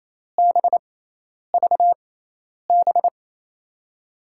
Morse code, heard as a keyed tone, about 28 wpm, spaced as for 10 wpm, 700 Hz. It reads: BVB
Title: USA Call Signs - 6 Characters - 6x Character Spacing - 28wpm